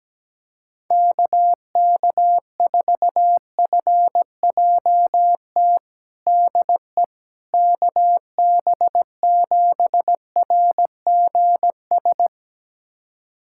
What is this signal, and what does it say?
Morse code, heard as a keyed tone, about 17 wpm, 700 Hz. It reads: KK4FJT DE KB7RGS
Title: USA Callsign DE Callsign - 6 Characters - 17wpm